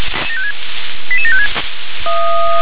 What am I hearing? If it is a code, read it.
Five-tone selective call (ZVEI), heard as quick runs of five tones, tones of about 80 ms. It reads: 88756
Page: SDR/ZVEI